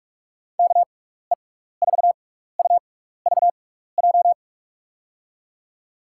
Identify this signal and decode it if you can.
Morse code, heard as a keyed tone, about 44 wpm, spaced as for 15 wpm, 700 Hz. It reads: KE4UVJ
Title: USA Call Signs - 6 Characters - 6x Character Spacing - 45wpm